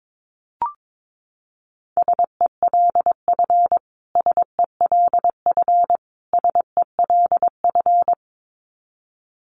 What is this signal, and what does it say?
Morse code, heard as a keyed tone, about 22 wpm, 700 Hz. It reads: SELF SELF SELF
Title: Top 500 Words Repeated 22wpm